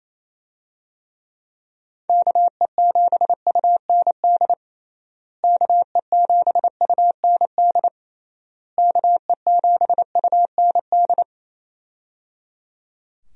Morse code, heard as a keyed tone, about 28 wpm, 700 Hz. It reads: KE7UND KE7UND KE7UND
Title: USA Call Signs - 6 Characters - Repeated 3x - 28wpm